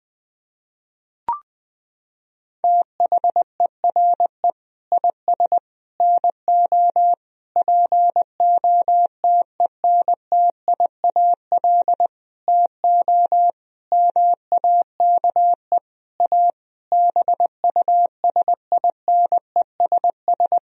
Morse code, heard as a keyed tone, about 20 wpm, 700 Hz. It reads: THERE IS NO POTENTIAL TO MAKE A BUSINESS
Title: Sentences from Top 1000 Words - Review Part 4 - 20wpm